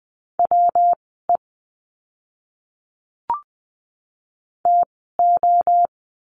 Morse code, heard as a keyed tone, about 20 wpm, spaced as for 14 wpm, 700 Hz. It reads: WE TO
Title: 192 Lesson 31f - Letter v - Words 20wpm